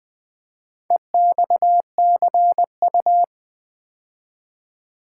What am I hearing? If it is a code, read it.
Morse code, heard as a keyed tone, about 21 wpm, 700 Hz. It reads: EXCU
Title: Build A Word - 2 Letters At A Time - 6-Letter Words - 20wpm